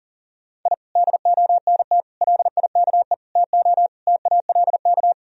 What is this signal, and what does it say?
Morse code, heard as a keyed tone, about 40 wpm, 700 Hz. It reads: I DONT LIKE TO TALK
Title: Sentences from Top 1000 Words - Review Part 1 - 40wpm